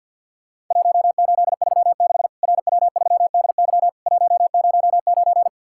Morse code, heard as a keyed tone, about 53 wpm, 700 Hz. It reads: 1836 RW3DY 109
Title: Worked All Europe DX Contest - QTC Exchange - 50wpm